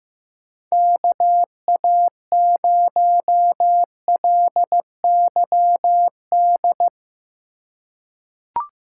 Morse code, heard as a keyed tone, about 15 wpm, 700 Hz. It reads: KA0LYD
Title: USA Call Signs - 6 Characters - 2x Character Spacing - 15wpm